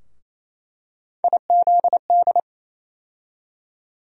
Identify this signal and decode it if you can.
Morse code, heard as a keyed tone, about 28 wpm, 700 Hz. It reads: IZD
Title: USA Call Sign Postfixes - 3 Letters 28wpm